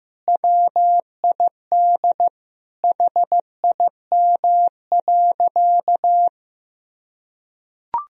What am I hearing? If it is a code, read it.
Morse code, heard as a keyed tone, about 16 wpm, 700 Hz. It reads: WID HIM.
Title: Sentences from Top 300 Words and Abbreviations 15wpm